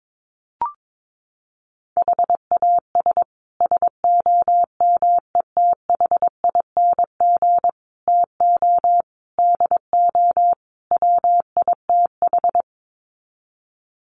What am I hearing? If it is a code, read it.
Morse code, heard as a keyed tone, about 22 wpm, 700 Hz. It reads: HAS SOMETHING TO DO WITH
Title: Sets of 5 Words Encore 22wpm